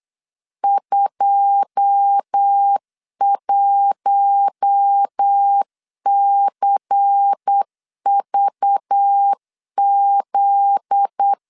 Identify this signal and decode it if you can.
Morse code, heard as a keyed tone, about 8 wpm, 800 Hz. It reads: 21CVZ
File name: LEZIONI / 9 2 8wpm 3437